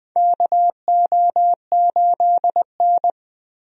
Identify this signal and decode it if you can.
Morse code, heard as a keyed tone, about 20 wpm, 700 Hz. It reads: KO8N
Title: USA Call Signs - 4 Characters - Repeated 3x - 20wpm